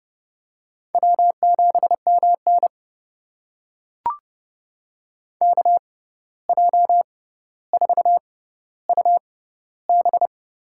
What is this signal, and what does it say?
Morse code, heard as a keyed tone, about 30 wpm, spaced as for 14 wpm, 700 Hz. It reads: W7MN KJ4UB
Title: USA Call Signs - 6x Character Spacing - 30wpm